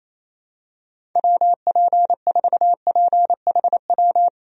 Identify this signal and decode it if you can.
Morse code, heard as a keyed tone, about 27 wpm, 700 Hz. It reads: WP4PHW
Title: USA Call Signs - 6 Characters - 28wpm